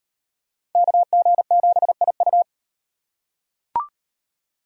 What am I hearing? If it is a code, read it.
Morse code, heard as a keyed tone, about 38 wpm, 700 Hz. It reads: KG7IU